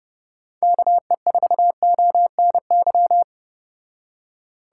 Morse code, heard as a keyed tone, about 30 wpm, 700 Hz. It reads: KE4ONY